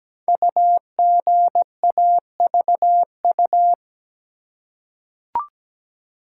Morse code, heard as a keyed tone, about 17 wpm, 700 Hz. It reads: UGAVU
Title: Code Groups - 5 Letter:Numbers - 17wpm